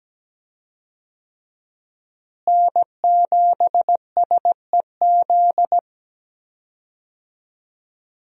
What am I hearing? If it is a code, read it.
Morse code, heard as a keyed tone, about 17 wpm, 700 Hz. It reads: N7SEZ
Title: USA Call Signs - 5 Characters - Repeated 2x - 17wpm